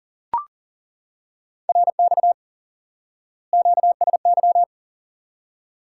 Morse code, heard as a keyed tone, about 40 wpm, 700 Hz. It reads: RX QSY
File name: IQE Course 3 - Lesson 89 - YAGI - Two Elements - 40wpm